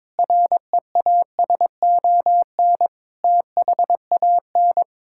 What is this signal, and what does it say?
Morse code, heard as a keyed tone, about 22 wpm, 700 Hz. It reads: REASON THAN
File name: practice / 22wpm / Sets of 6 Words - 22wpm